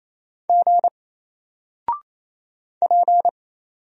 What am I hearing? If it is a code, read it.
Morse code, heard as a keyed tone, about 27 wpm, 700 Hz. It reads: G P
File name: Single-Letter-Number-28wpm